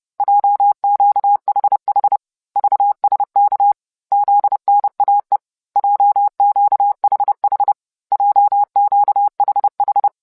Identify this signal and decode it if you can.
Morse code, heard as a keyed tone, about 30 wpm, 810 Hz. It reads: JQHH VSK ZNAE JQHH JQHH